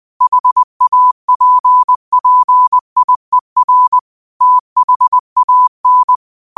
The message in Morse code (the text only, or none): HAPPIER THAN